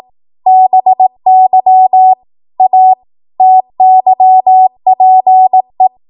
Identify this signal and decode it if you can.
Morse code, heard as a keyed tone, about 18 wpm, 750 Hz. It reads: BY A TYPE